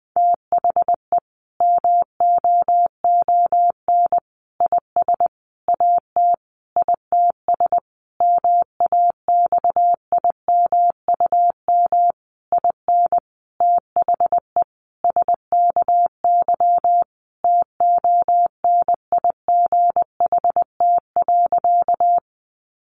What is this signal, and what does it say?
Morse code, heard as a keyed tone, about 20 wpm, 700 Hz. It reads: THE MOON IS AT ITS MAXIMUM IN THE SKY TONIGHT.
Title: Sentences from Top 1300 Words 20wpm